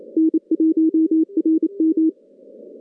Morse code, heard as a keyed tone, about 27 wpm, 320 Hz. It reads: N1RM
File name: contests / CW-SS-k5tr- -2100